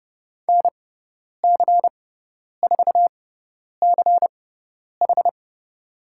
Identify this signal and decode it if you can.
Morse code, heard as a keyed tone, about 30 wpm, spaced as for 10 wpm, 700 Hz. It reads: NC4CH
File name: Lesson 17g - Letter u - Call Signs 30wpm